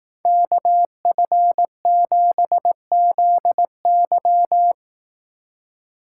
Morse code, heard as a keyed tone, about 18 wpm, 700 Hz. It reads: KF7ZY
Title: Speed Racing - USA 2x2 Call Signs 15wpm